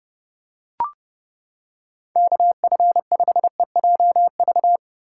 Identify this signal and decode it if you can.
Morse code, heard as a keyed tone, about 30 wpm, 700 Hz. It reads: KF5EJV